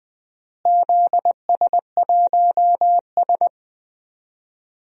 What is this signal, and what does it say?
Morse code, heard as a keyed tone, about 19 wpm, 700 Hz. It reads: ZS1S
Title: DXCC-100 Call Signs - 4 Characters - Repeated 2x - 20wpm